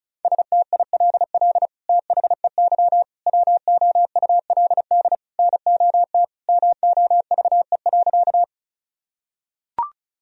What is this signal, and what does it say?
Morse code, heard as a keyed tone, about 36 wpm, 700 Hz. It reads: STILL THEY WOULD NOT MOVE.
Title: Sentences from Top 1000 Words - Review Part 1 - 35wpm